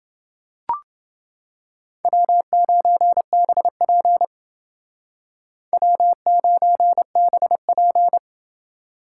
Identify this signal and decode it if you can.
Morse code, heard as a keyed tone, about 28 wpm, 700 Hz. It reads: W9BP W9BP